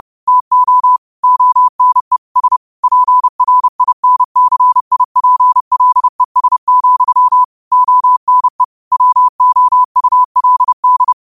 Morse code, heard as a keyed tone, about 30 wpm, 1000 Hz. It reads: TO ONE S PRINCIPLES, ONE WOULD
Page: Quotes for Sun, 17 Aug 2025 in Morse Code at 30 words per minute.